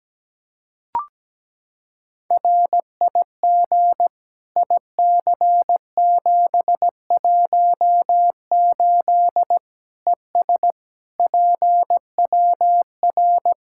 Morse code, heard as a keyed tone, about 17 wpm, 700 Hz. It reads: RIG IC718 ES PWR
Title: Practice QSO - Exchange 2 - Info - 17wpm